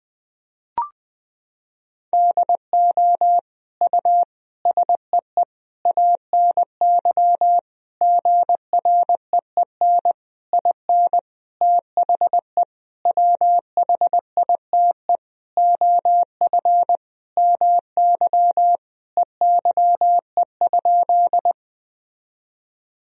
Morse code, heard as a keyed tone, about 20 wpm, 700 Hz. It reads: DO U SEE ANY GREEN IN THE WHITE OF MY EYE?
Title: Sentences from Top 500 Words and Abbreviations 20wpm